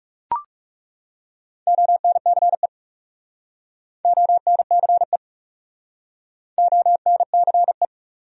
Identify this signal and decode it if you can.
Morse code, heard as a keyed tone, about 37 wpm, 700 Hz. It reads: ONCE ONCE ONCE
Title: Speed-Racing Top 500 Words 30wpm